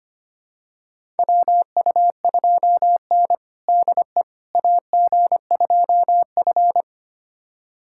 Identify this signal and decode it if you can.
Morse code, heard as a keyed tone, about 25 wpm, 700 Hz. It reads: WU2N DE AG2F
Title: USA Callsign DE Callsign - 4 Characters - 25wpm